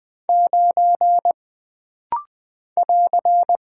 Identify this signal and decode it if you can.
Morse code, heard as a keyed tone, about 20 wpm, 700 Hz. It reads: 9 <AR>
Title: Single Character (Letter:Number:Punctuation:Prosign) - Rapid-Fire - 20wpm